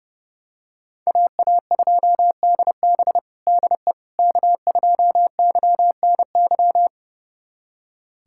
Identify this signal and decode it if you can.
Morse code, heard as a keyed tone, about 32 wpm, 700 Hz. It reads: AA2DB DE K2YNY
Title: USA Callsign DE Callsign - 5 Characters - 30wpm